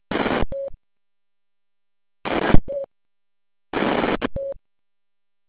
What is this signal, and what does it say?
There is no code.